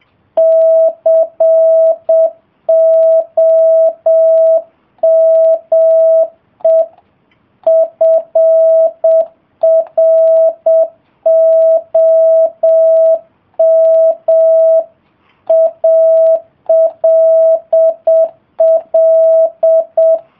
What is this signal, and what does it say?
Morse code, heard as a keyed tone, about 7 wpm, 660 Hz. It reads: COME FROM ALL